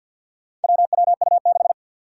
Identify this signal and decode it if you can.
Morse code, heard as a keyed tone, about 51 wpm, 700 Hz. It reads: WWU6
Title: Four Challenging Letter-Number Combinations 50wpm